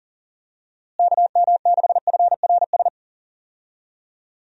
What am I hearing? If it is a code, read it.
Morse code, heard as a keyed tone, about 40 wpm, 700 Hz. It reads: KM6FRS
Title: USA Call Signs - 6 Characters - Repeated 3x - 40wpm